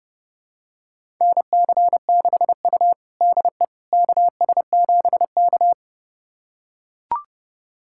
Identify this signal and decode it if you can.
Morse code, heard as a keyed tone, about 31 wpm, 700 Hz. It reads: NC6U DE KS7K
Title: USA Callsign DE Callsign - 4 Characters - 30wpm